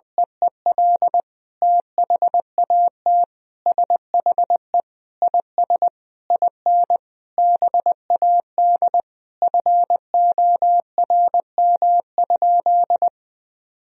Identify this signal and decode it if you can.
Morse code, heard as a keyed tone, about 21 wpm, 700 Hz. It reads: EEL THAT SHE IS IN BAD FORM?
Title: Sentences from Top 1000 Words - Review Part 1 - 20wpm